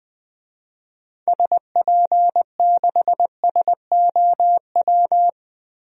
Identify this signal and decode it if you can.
Morse code, heard as a keyed tone, about 20 wpm, 700 Hz. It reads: SP6SOW
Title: DXCC-100 Call Signs - 6 Characters - 20wpm